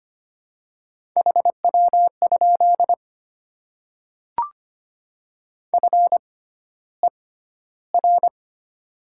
Morse code, heard as a keyed tone, about 25 wpm, spaced as for 13 wpm, 700 Hz. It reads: HW? FER
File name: Top 20 QSO Elements - 6x Character Spacing - 25wpm